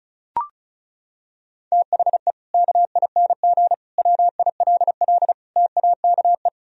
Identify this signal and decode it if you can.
Morse code, heard as a keyed tone, about 36 wpm, 700 Hz. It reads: THE KING WILL TAKE
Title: Sentences from Top 1000 Words - Review Part 4 - 35wpm